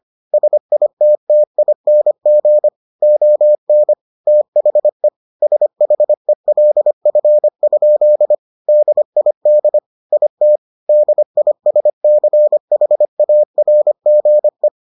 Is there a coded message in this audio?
Yes: Morse code from 0.3 to 14.7 s, heard as a keyed tone, about 25 wpm, 600 Hz. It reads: SITTING ON THE SHELF? DID IT DISCHARGE